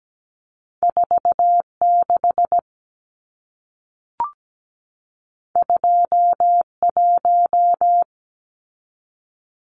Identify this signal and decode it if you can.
Morse code, heard as a keyed tone, about 17 wpm, 700 Hz. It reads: 46 21
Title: Numbers - 2-Digit 17wpm